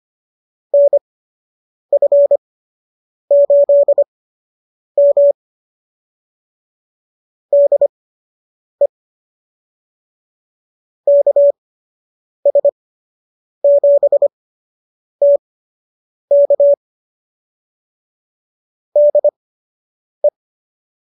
Morse code, heard as a keyed tone, about 25 wpm, spaced as for 8 wpm, 570 Hz. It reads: NF8M DE KS7T K DE